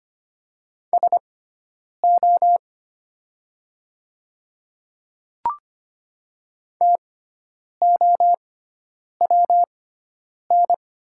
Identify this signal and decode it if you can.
Morse code, heard as a keyed tone, about 25 wpm, spaced as for 9 wpm, 700 Hz. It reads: SO TOWN
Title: 6x Character Spacing - Top 500 Words - 25wpm